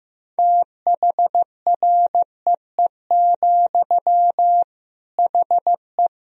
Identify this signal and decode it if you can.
Morse code, heard as a keyed tone, about 15 wpm, 700 Hz. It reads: THREE, HE
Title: Sentences from Top 200 Words and Abbreviations 15wpm